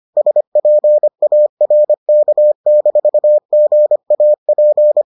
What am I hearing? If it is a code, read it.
Morse code, heard as a keyed tone, about 25 wpm, 600 Hz. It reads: SPARK-GAP